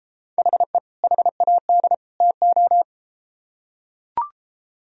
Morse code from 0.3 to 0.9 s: HE

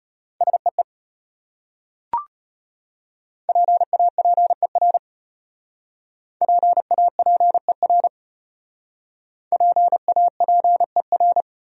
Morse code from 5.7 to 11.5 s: PAPER PAPER

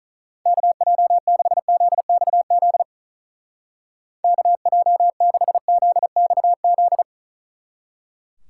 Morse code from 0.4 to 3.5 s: KJ6ZXZ

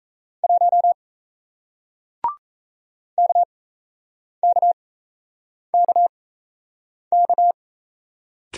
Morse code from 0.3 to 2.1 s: 1

